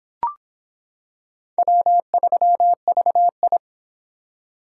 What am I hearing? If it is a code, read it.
Morse code, heard as a keyed tone, about 26 wpm, 700 Hz. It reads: W3VI